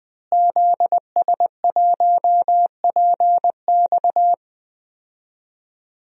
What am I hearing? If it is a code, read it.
Morse code, heard as a keyed tone, about 20 wpm, 700 Hz. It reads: ZS1PX
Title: DXCC-100 Call Signs - 5 Characters - Repeated 2x - 20wpm